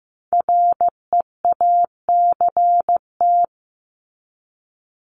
Morse code, heard as a keyed tone, about 15 wpm, 700 Hz. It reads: REACT
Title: Top 1251-1500 Words 15wpm